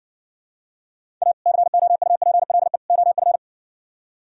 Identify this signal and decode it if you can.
Morse code, heard as a keyed tone, about 69 wpm, 700 Hz. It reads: A COUPLE OF